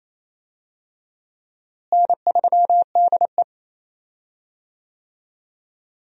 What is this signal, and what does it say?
Morse code, heard as a keyed tone, about 28 wpm, 700 Hz. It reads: N3DE
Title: USA Call Signs - 4 Characters - Repeated 3x - 28wpm